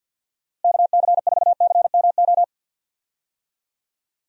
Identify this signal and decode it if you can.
Morse code, heard as a keyed tone, about 48 wpm, 700 Hz. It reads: KC4CMO
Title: USA Call Signs - 6 Characters - Repeated 3x - 50wpm